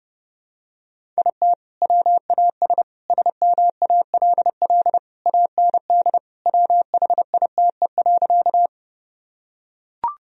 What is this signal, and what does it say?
Morse code, heard as a keyed tone, about 30 wpm, 700 Hz. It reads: IT WAS SMALL AND WHITE.
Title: Sentences from Top 1000 Words - Review Part 1 - 30wpm